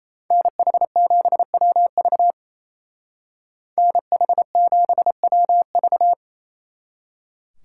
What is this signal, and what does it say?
Morse code, heard as a keyed tone, about 33 wpm, 700 Hz. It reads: NH7WV NH7WV
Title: Speed Racing - USA 2x2 Call Signs 28wpm